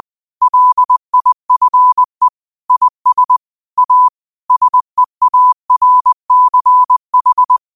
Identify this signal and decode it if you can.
Morse code, heard as a keyed tone, about 20 wpm, 1000 Hz. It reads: LIFE IS A SEARCH
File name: qotdfiles / 20wpm